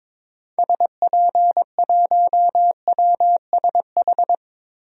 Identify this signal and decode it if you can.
Morse code, heard as a keyed tone, about 22 wpm, 700 Hz. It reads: SP1WSH
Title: DXCC-100 Call Signs - 6 Characters - Repeated 3x - 22wpm